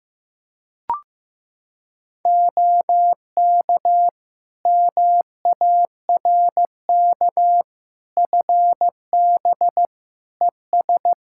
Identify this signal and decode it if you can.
Morse code, heard as a keyed tone, about 15 wpm, 700 Hz. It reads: OK MARK FB ES
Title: Practice QSO - Exchange 2 - Info - 15wpm